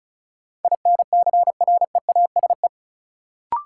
Morse code, heard as a keyed tone, about 36 wpm, 700 Hz. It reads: INCREASE
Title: Sets of 5 Words - New Word Drill - 35wpm